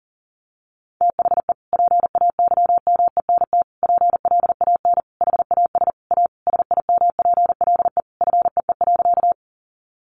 Morse code, heard as a keyed tone, about 40 wpm, 700 Hz. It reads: THE PAYMENT PLAN HAS A SIMPLE FEE.
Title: Sentences from Top 1500 Words 40wpm